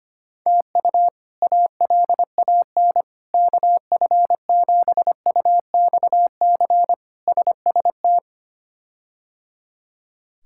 Morse code, heard as a keyed tone, about 25 wpm, 700 Hz. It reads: TU ALAN KF7UXC SST
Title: K1USN SST QSO Practice - 25wpm